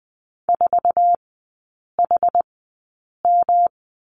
Morse code, heard as a keyed tone, about 20 wpm, spaced as for 8 wpm, 700 Hz. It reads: 4HM
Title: 107 Lesson 18e - Letter m - Three Characters 20wpm